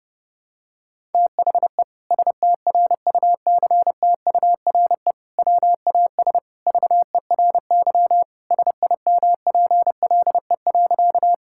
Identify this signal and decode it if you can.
Morse code, heard as a keyed tone, about 28 wpm, 700 Hz. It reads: THE STRUCTURE WAS VERY SIMPLE.
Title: Sentences from Top 1000 Words - Review Part 6 - 30wpm